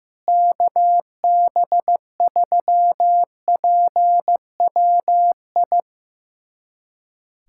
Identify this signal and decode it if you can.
Morse code, heard as a keyed tone, about 15 wpm, 700 Hz. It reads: KB3PWI
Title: USA Callsign DE Callsign 15wpm